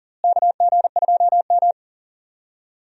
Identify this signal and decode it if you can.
Morse code, heard as a keyed tone, about 40 wpm, 700 Hz. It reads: KG2M
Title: USA Call Signs - 4 Characters - Repeated 2x - 40wpm